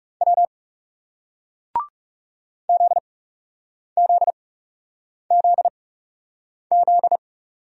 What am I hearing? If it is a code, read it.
Morse code, heard as a keyed tone, about 40 wpm, spaced as for 8 wpm, 700 Hz. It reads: W ZZZZ